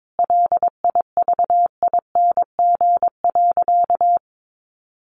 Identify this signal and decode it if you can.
Morse code, heard as a keyed tone, about 22 wpm, 700 Hz. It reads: LIVING.
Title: Sentences from Top 1500 Words 22wpm